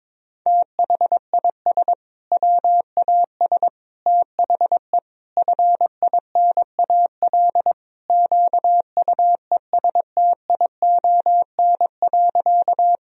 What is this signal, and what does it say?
Morse code, heard as a keyed tone, about 22 wpm, 700 Hz. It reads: THIS WAS THE FINAL QUESTION.
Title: Sentences from Top 1000 Words - Review Part 2 - 22wpm